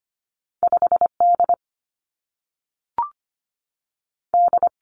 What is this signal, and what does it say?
Morse code, heard as a keyed tone, about 25 wpm, 700 Hz. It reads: 5 D D
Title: Two Challenging Letter-Number Combinations 25wpm